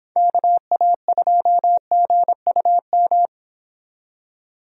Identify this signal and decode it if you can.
Morse code, heard as a keyed tone, about 26 wpm, 700 Hz. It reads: KA2GUM